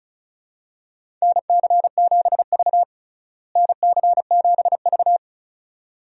Morse code, heard as a keyed tone, about 36 wpm, 700 Hz. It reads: NC7V NC7V
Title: USA Call Signs - 4 Characters - Repeated 2x - 35wpm